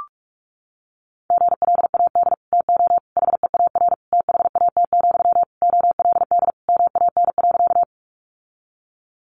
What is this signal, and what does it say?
Morse code, heard as a keyed tone, about 44 wpm, 700 Hz. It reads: GLAD TO HEAR THAT, OLD MAN.
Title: Sentences from Top 800 Words 45wpm